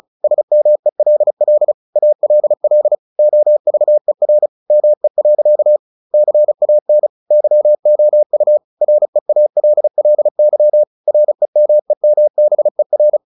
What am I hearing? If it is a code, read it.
Morse code, heard as a keyed tone, about 34 wpm, 600 Hz. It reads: SMELL ALL OVER ME. CAN YOU REALLY REMEMBER
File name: the old man and the sea / the old man and the sea 1 35wpm